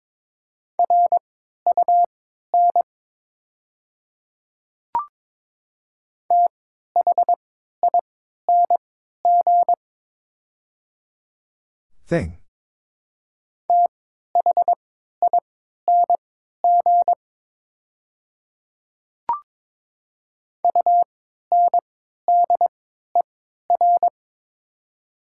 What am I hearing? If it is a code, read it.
Morse code, heard as a keyed tone, about 22 wpm, spaced as for 12 wpm, 700 Hz. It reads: RUN THING THING UNDER
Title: 3x Character Spacing - Top 300 Words - 22wpm